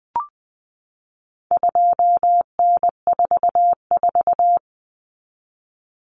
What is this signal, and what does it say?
Morse code, heard as a keyed tone, about 20 wpm, 700 Hz. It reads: 2N44
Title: Common Cut Numbers - 4-Digit 20wpm